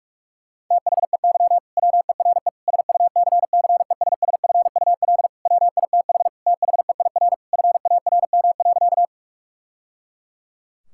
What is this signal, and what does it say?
Morse code, heard as a keyed tone, about 41 wpm, 700 Hz. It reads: THEY WERE SUCCESSFUL WITH THEIR FARM.
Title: Sentences from Top 1000 Words - Review Part 6 - 45wpm